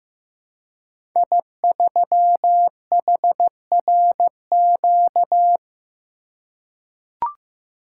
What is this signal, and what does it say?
Morse code, heard as a keyed tone, about 15 wpm, 700 Hz. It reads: I3HRQ
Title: Code Groups - 5 Letter:Numbers - 15wpm